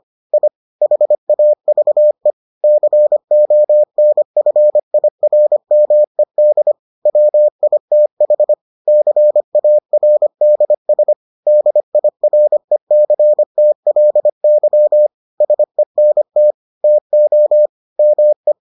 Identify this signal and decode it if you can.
Morse code, heard as a keyed tone, about 25 wpm, 600 Hz. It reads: I HAVE CONFIRMED WITH CARDS DIRECTLY SENT TO ME